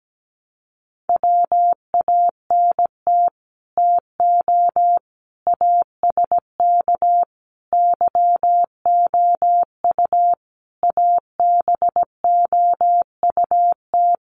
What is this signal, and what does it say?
Morse code, heard as a keyed tone, about 17 wpm, 700 Hz. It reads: WANT TO ASK YOU ABOUT
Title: Sets of 5 Words Encore2 17wpm